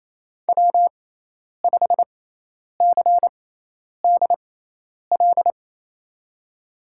Morse code, heard as a keyed tone, about 28 wpm, spaced as for 10 wpm, 700 Hz. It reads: W5CDL
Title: USA Call Signs - 6x Character Spacing - 28wpm